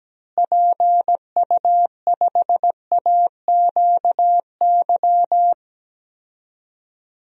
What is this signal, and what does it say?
Morse code, heard as a keyed tone, about 17 wpm, 700 Hz. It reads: PU5AQY
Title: DXCC-100 Call Signs - 6 Characters - Repeated 2x - 17wpm